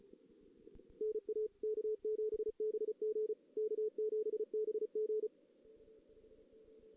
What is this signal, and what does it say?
Morse code, heard as a keyed tone, about 34 wpm, 440 Hz. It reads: NA K7BG K7BG
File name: 2016 JAN CW NAQP / 2016 Jan naqp k5ot - 0-2